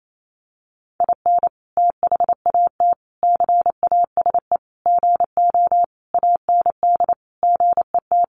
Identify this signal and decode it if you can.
Morse code, heard as a keyed tone, about 28 wpm, 700 Hz. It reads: IN THAT CASE GO AND GET
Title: Sentences from Top 700 Words 28wpm